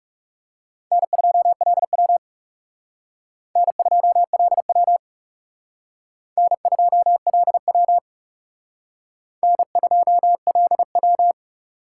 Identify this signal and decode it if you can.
Morse code, heard as a keyed tone, about 38 wpm, 700 Hz. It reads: N2LW N2LW N2LW N2LW